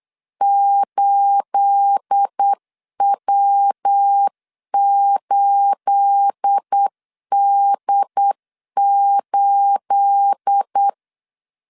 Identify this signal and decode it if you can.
Morse code, heard as a keyed tone, about 8 wpm, 800 Hz. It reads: 8W8D8
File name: LEZIONI / 9 28 8wpm 3512